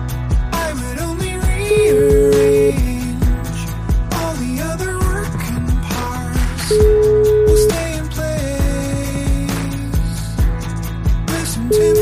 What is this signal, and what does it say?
Ringback tone (Europe, 425 Hz), with noise.